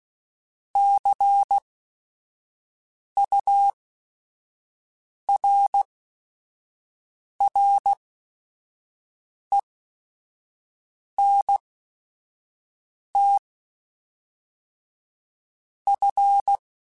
Morse code, heard as a keyed tone, about 16 wpm, spaced as for 5 wpm, 800 Hz. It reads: CURRENT F
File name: Technical Subjects / Technical Subjects-mp3s / Reactance @5